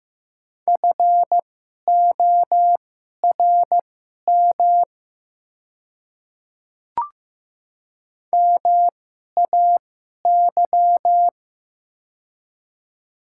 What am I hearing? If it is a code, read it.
Morse code, heard as a keyed tone, about 15 wpm, spaced as for 11 wpm, 700 Hz. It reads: FORM MAY
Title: 2x Character Spacing - Top 300 Words - 15wpm